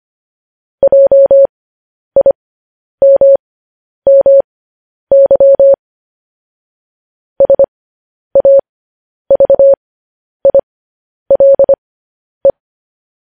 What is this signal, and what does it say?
Morse code, heard as a keyed tone, about 25 wpm, spaced as for 10 wpm, 550 Hz. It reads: JIMMY SAVILE
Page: news headlines in morse code at 10 wpm